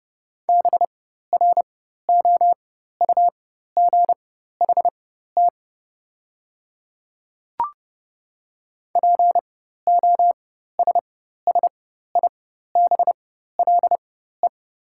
Morse code, heard as a keyed tone, about 30 wpm, spaced as for 14 wpm, 700 Hz. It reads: BROUGHT POSSIBLE
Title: 4x Character Spacing - Top 500 Words - 30wpm